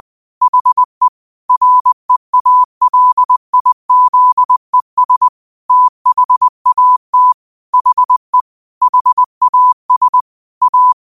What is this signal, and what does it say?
Morse code, heard as a keyed tone, about 20 wpm, 1000 Hz. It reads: HE REALIZES THAT HE HAS A